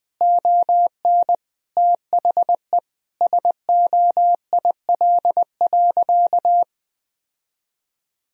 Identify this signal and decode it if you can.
Morse code, heard as a keyed tone, about 21 wpm, 700 Hz. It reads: ON THE SOIL.
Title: Sentences from Top 1000 Words - Review Part 4 - 20wpm